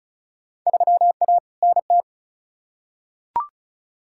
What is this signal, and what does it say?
Morse code, heard as a keyed tone, about 36 wpm, 700 Hz. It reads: 3A NT
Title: Field Day Exchanges Encore - 35wpm